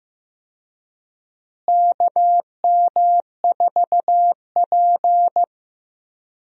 Morse code, heard as a keyed tone, about 15 wpm, 700 Hz. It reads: KM4P
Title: USA Call Signs - 4 Characters - Repeated 3x - 15wpm